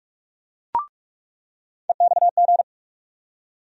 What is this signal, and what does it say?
Morse code, heard as a keyed tone, about 46 wpm, 700 Hz. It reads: EXG
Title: USA Call Sign Postfixes - 3 Letters 45wpm